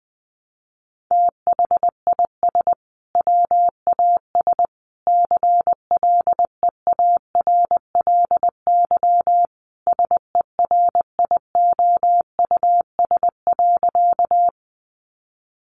Morse code, heard as a keyed tone, about 19 wpm, 700 Hz. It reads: THIS WAS CLEARLY SERIOUS.
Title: Sentences from Top 1300 Words 20wpm